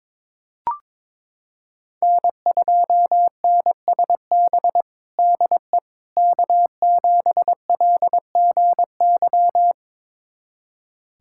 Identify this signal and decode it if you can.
Morse code, heard as a keyed tone, about 21 wpm, 700 Hz. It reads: N2NSB DE K7LGY
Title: USA Callsign DE Callsign - 5 Characters - 22wpm